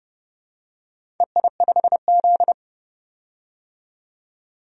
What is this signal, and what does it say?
Morse code, heard as a keyed tone, about 32 wpm, 700 Hz. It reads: EI5Z
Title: DXCC-100 Call Signs - 4 Characters - 30wpm